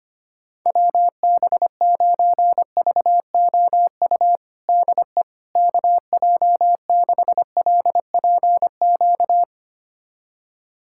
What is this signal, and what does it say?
Morse code, heard as a keyed tone, about 25 wpm, 700 Hz. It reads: WB9VOU DE KJ6LPQ
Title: USA Callsign DE Callsign - 6 Characters - 25wpm